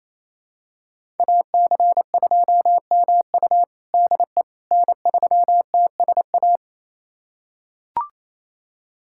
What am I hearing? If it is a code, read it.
Morse code, heard as a keyed tone, about 28 wpm, 700 Hz. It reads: AC2MU DE N3TSA
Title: USA Callsign DE Callsign - 5 Characters - 28wpm